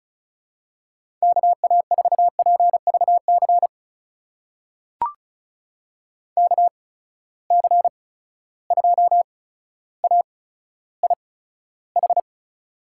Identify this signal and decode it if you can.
Morse code, heard as a keyed tone, about 36 wpm, spaced as for 15 wpm, 700 Hz. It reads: KA4PVC KC2AIH
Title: USA Call Signs - 8x Character Spacing - 35wpm